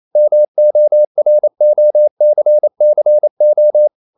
Morse code, heard as a keyed tone, about 28 wpm, 600 Hz. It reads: MOROCCO
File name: CW / Practice / NationalCapitals28